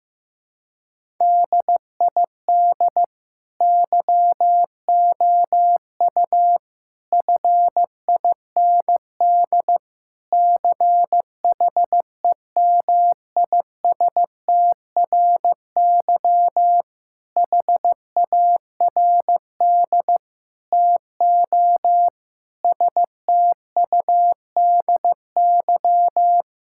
Morse code, heard as a keyed tone, about 16 wpm, 700 Hz. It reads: DID YOU FIND CHEMISTRY HARD TO STUDY
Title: Sentences from Top 1000 Words - Review Part 6 - 15wpm